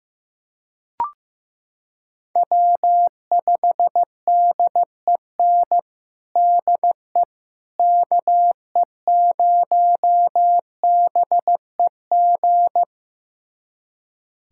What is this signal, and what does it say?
Morse code, heard as a keyed tone, about 16 wpm, 700 Hz. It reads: W5DEN DE KE0BEG
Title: USA Callsign DE Callsign 15wpm